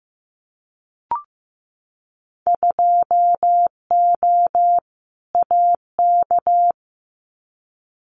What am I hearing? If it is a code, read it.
Morse code, heard as a keyed tone, about 15 wpm, 700 Hz. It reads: 2O AK
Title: Winter Field Day Exchanges Encore 15wpm